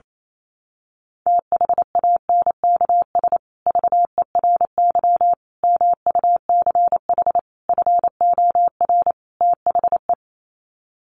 Morse code, heard as a keyed tone, about 28 wpm, 700 Hz. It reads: THANKS VERY MUCH FOR THE
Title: Sets of 5 Words Encore2 28wpm